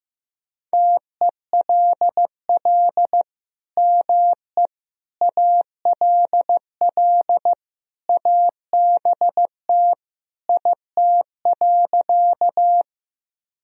Morse code, heard as a keyed tone, about 15 wpm, 700 Hz. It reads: TELL ME ALL ABT IT.
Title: Sentences from Top 200 Words and Abbreviations 15wpm